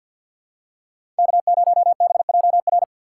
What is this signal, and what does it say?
Morse code, heard as a keyed tone, about 53 wpm, 700 Hz. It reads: K0BJR
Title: USA Call Signs - 4x Character Spacing - 50wpm